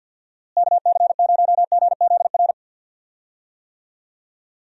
Morse code, heard as a keyed tone, about 50 wpm, 700 Hz. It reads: KC0GZR
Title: USA Call Signs - 6 Characters - Repeated 2x - 50wpm